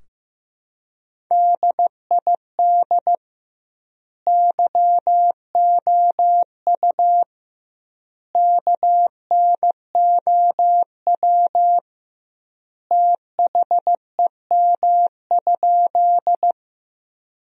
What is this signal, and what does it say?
Morse code, heard as a keyed tone, about 15 wpm, 700 Hz. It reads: DID YOU KNOW THEM?
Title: 2x Word-Spacing - Sentences from Top 100 Words - 15wpm